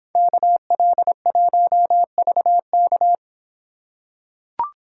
Morse code, heard as a keyed tone, about 26 wpm, 700 Hz. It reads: KL1VK